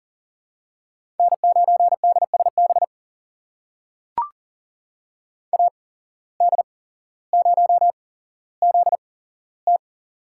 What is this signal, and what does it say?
Morse code, heard as a keyed tone, about 40 wpm, spaced as for 17 wpm, 700 Hz. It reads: N9DSB AD0ZT
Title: USA Call Signs - 5 Characters - 8x Character Spacing - 40wpm